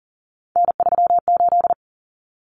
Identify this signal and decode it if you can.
Morse code, heard as a keyed tone, about 40 wpm, 700 Hz. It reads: N38
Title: Common Cut Numbers - 3-Digit 40wpm